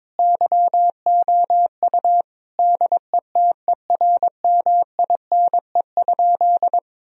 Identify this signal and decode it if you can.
Morse code, heard as a keyed tone, about 22 wpm, 700 Hz. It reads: YOU DETERMINE?
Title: Sentences from Top 1000 Words - Review Part 4 - 22wpm